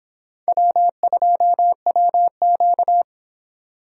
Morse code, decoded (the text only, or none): W2WQ